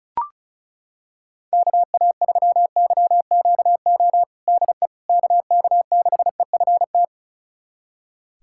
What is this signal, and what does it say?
Morse code, heard as a keyed tone, about 36 wpm, 690 Hz. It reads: KA3YQO DE KK6EFT
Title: USA Callsign DE Callsign - 6 Characters - 35wpm